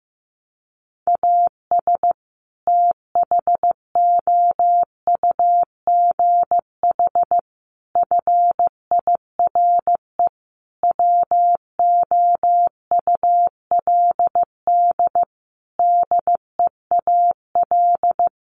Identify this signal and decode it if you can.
Morse code, heard as a keyed tone, about 15 wpm, 700 Hz. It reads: AS THOUGH FIRE WOULD DEAL
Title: Sentences from Top 1500 Words 15wpm